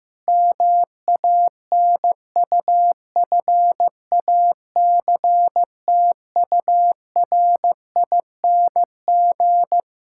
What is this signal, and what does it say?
Morse code, heard as a keyed tone, about 16 wpm, 700 Hz. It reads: MANUFACTURING
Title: K1USN SST QSO Practice 15wpm